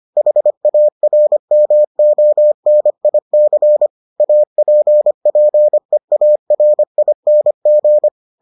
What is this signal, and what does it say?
Morse code, heard as a keyed tone, about 25 wpm, 600 Hz. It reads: HARMONIC APPEARING